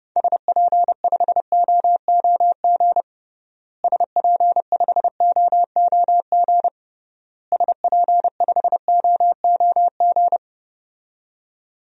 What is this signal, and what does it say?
Morse code, heard as a keyed tone, about 32 wpm, 700 Hz. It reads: SP5OOG SP5OOG SP5OOG
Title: DXCC-100 Call Signs - 6 Characters - Repeated 3x - 30wpm